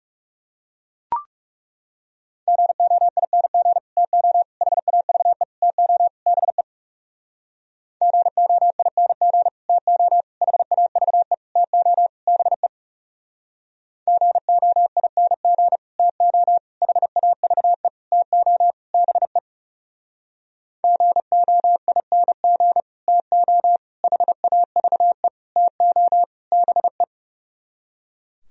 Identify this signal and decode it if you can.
Morse code, heard as a keyed tone, about 36 wpm, 700 Hz. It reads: GOING TO HAVE TO BE GOING TO HAVE TO BE GOING TO HAVE TO BE GOING TO HAVE TO BE
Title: Speed-Racing - Sets of 5 Words 30wpm